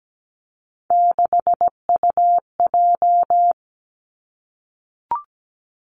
Morse code, heard as a keyed tone, about 17 wpm, 700 Hz. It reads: 6UJ
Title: Three Challenging Letter-Number Combinations 17wpm